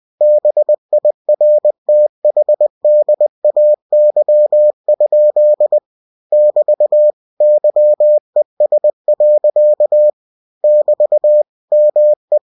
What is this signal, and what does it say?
Morse code, heard as a keyed tone, about 20 wpm, 600 Hz. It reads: BIRTHDAY? <BT> YES. <BT> ME
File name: cw-books / winnie-the-pooh / winnie-20wpm-page-0054